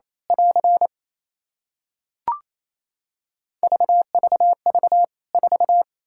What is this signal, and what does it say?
Morse code, heard as a keyed tone, about 28 wpm, 700 Hz. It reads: <AR> VVV 4
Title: K1USN SST QSO Practice 28wpm